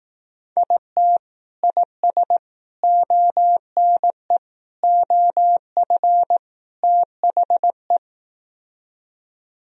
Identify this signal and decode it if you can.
Morse code, heard as a keyed tone, about 18 wpm, 700 Hz. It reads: IT IS ONE OF THE
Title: Speed-Racing - Sets of 5 Words 15wpm